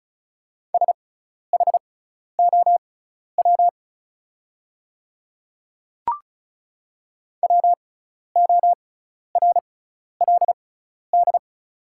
Morse code, heard as a keyed tone, about 36 wpm, spaced as for 12 wpm, 700 Hz. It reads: SHOW WORLD
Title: 6x Character Spacing - Top 200 Words - 35wpm